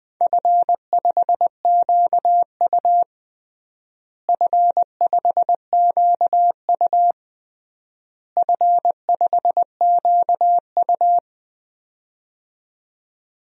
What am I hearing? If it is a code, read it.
Morse code, heard as a keyed tone, about 19 wpm, 700 Hz. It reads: F5QU F5QU F5QU
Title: DXCC-100 Call Signs - 4 Characters - Repeated 3x - 20wpm